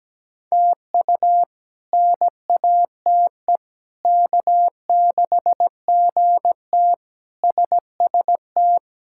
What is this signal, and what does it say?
Morse code, heard as a keyed tone, about 17 wpm, 700 Hz. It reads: TU NATE K6GT SST
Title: K1USN SST QSO Practice - 17wpm